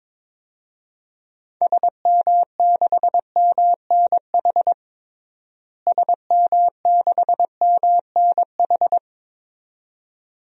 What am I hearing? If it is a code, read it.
Morse code, heard as a keyed tone, about 22 wpm, 700 Hz. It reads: SM6MNH SM6MNH
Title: DXCC-100 Call Signs - 6 Characters - Repeated 2x - 22wpm